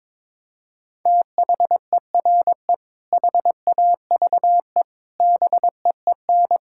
Morse code, heard as a keyed tone, about 22 wpm, 700 Hz. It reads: THERE HAVE BEEN
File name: practice / 22wpm / Sets of 6 Words - 22wpm